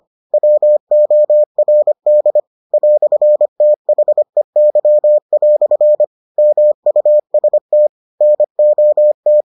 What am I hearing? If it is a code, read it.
Morse code, heard as a keyed tone, about 25 wpm, 600 Hz. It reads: WORD "THEY" MUST NOT